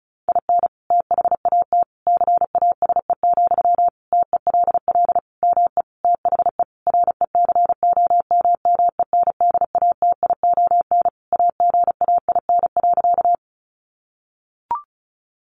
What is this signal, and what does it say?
Morse code, heard as a keyed tone, about 34 wpm, 700 Hz. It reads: IN THAT CASE, TELL ME THE RECOMMENDATION AGAIN.
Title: Sentences from Top 1400 Words 35wpm